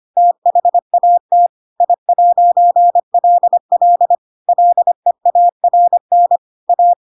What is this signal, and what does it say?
Morse code, heard as a keyed tone, about 25 wpm, 700 Hz. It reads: THAT I'LL LEARN A